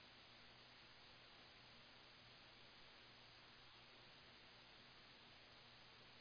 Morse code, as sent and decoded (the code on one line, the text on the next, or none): none